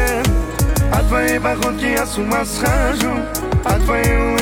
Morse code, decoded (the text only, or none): none